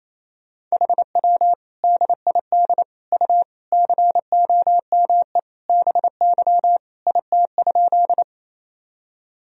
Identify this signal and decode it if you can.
Morse code, heard as a keyed tone, about 28 wpm, 700 Hz. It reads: HW DID U COME BY IT?
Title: Sentences from Top 100 Words and Abbreviations 28wpm